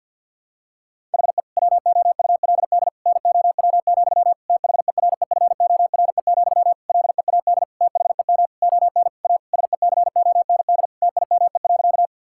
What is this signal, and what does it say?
Morse code, heard as a keyed tone, about 52 wpm, 700 Hz. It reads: HE WOULD NOW, THEREFORE, LEAD THEM ON A SECOND TIME.
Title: Sentences from Top 1000 Words - Review Part 5 - 50wpm